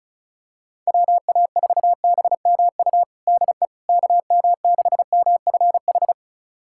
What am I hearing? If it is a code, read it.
Morse code, heard as a keyed tone, about 34 wpm, 690 Hz. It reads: WA4BMU DE KM6MFH
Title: USA Callsign DE Callsign 35wpm